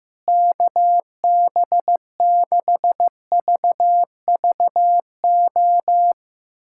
Morse code, heard as a keyed tone, about 15 wpm, 700 Hz. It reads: KB6VVO
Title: USA Callsign DE Callsign 15wpm